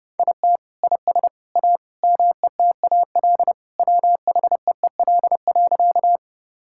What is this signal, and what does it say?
Morse code, heard as a keyed tone, about 32 wpm, 700 Hz. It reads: IT IS A METAL WHEEL.
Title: Sentences from Top 1000 Words - Review Part 6 - 30wpm